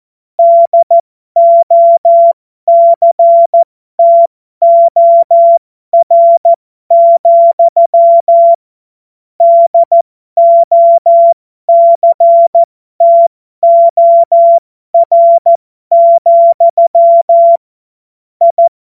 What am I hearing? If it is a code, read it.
Morse code, heard as a keyed tone, about 14 wpm, spaced as for 12 wpm, 680 Hz. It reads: DOCTOR, DOCTOR, I
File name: Morsekurs / Witze